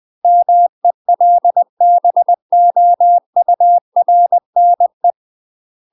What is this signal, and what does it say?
Morse code, heard as a keyed tone, about 20 wpm, 700 Hz. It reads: MELBOURNE
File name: UK MASTER QSOs TRAINING/20WPM